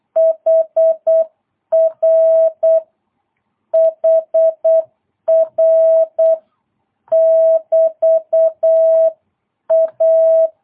Morse code, heard as a keyed tone, about 8 wpm, 660 Hz. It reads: HR HR <BT> A
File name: jh8cbh / 2022cw